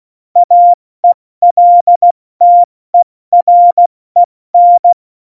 Morse code, heard as a keyed tone, about 16 wpm, 700 Hz. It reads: AELTEREN
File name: Morsekurs / Witze / Witz 04. Juni 2019, 70 F80